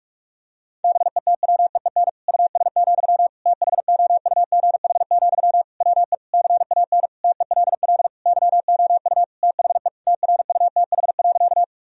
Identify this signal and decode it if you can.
Morse code, heard as a keyed tone, about 46 wpm, 700 Hz. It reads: BETWEEN US, THOUGH, WE CAN TELL YOU THE TRUTH.
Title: Sentences from Top 1000 Words - Review Part 2 - 45wpm